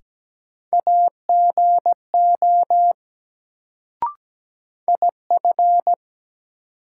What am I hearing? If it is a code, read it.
Morse code, heard as a keyed tone, about 17 wpm, 700 Hz. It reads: AGO IF
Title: Sets of 2 Words - New Word Drill - 17wpm